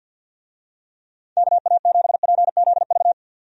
Morse code, heard as a keyed tone, about 52 wpm, 700 Hz. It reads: KA7PZV
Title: USA Call Signs - 6 Characters - 50wpm